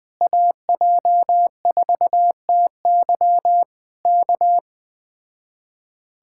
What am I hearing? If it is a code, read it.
Morse code, heard as a keyed tone, about 20 wpm, 700 Hz. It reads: AJ4TY K
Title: Speed-Racing Common QSO Phrases Encore - 17wpm